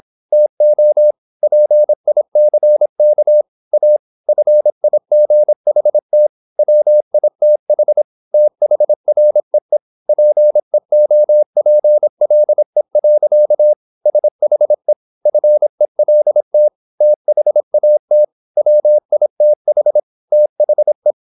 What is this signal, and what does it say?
Morse code, heard as a keyed tone, about 26 wpm, 600 Hz. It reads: TO PICK A FIGHT WITH THREE PEOPLE. SHE FELT THAT WITH THE
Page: Never 26wpm | CW med Gnister